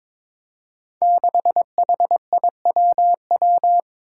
Morse code, heard as a keyed tone, about 23 wpm, 700 Hz. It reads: 6HIWW
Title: Five Challenging Letter-Number Combinations 22wpm